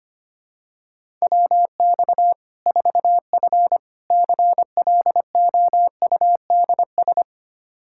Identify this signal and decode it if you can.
Morse code, heard as a keyed tone, about 25 wpm, 700 Hz. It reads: WX 4F CLOUDS
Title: Practice QSO - Exchange 2 - Info - 25wpm